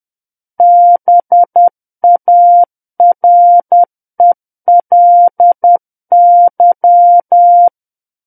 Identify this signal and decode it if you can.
Morse code, heard as a keyed tone, about 10 wpm, 700 Hz. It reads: BARELY
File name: Book One / 01-Chapter 1 - 10 WPM